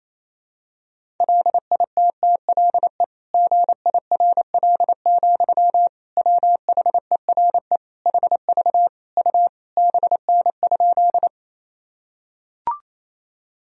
Morse code, heard as a keyed tone, about 27 wpm, 700 Hz. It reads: LITTLE GIRL, WHERE HV U BN?
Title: Sentences from Top 400 Words and Abbreviations 28wpm